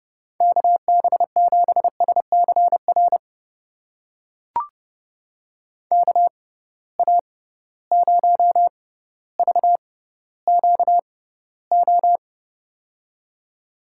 Morse code, heard as a keyed tone, about 30 wpm, spaced as for 15 wpm, 700 Hz. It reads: KB7SCR KA0VQO
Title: USA Call Signs - 6x Character Spacing - 30wpm